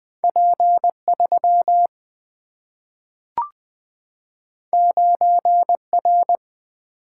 Morse code, heard as a keyed tone, about 20 wpm, 700 Hz. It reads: P 3 9 R
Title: Single-Letter Single-Number 20wpm